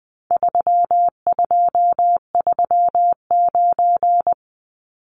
Morse code, heard as a keyed tone, about 20 wpm, 700 Hz. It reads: 3239
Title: Numbers - 4-Digit 20wpm